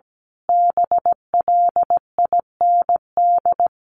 Morse code, heard as a keyed tone, about 17 wpm, 700 Hz. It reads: BLIND
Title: A Sound Of Thunder 17wpm